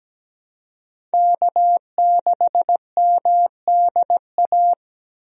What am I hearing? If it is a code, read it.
Morse code, heard as a keyed tone, about 17 wpm, 700 Hz. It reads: K6MDA
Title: USA Call Signs - 5 Characters - Repeated 3x - 17wpm